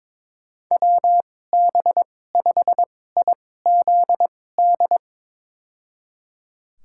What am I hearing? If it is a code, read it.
Morse code, heard as a keyed tone, about 22 wpm, spaced as for 16 wpm, 700 Hz. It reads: WB5IZD
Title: USA Call Signs - 2x Character Spacing - 22wpm